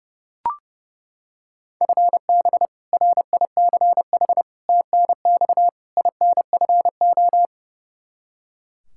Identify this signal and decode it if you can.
Morse code, heard as a keyed tone, about 30 wpm, 700 Hz. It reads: FB RICH TNX INFO
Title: Practice QSO - Exchange 3 - More Info - 30wpm